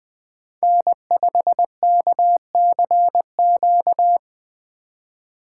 Morse code, heard as a keyed tone, about 20 wpm, 700 Hz. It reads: N5KCQ
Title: USA Call Signs - 3x Character Spacing - 20wpm